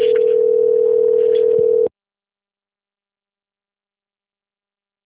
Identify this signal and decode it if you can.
Ringback tone (North America), 440 + 480 Hz.